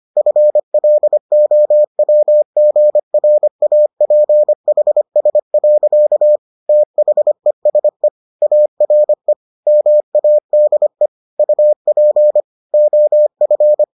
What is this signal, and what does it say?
Morse code, heard as a keyed tone, about 25 wpm, 600 Hz. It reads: FLOWGRAPHS. THESE ARE MADE UP OF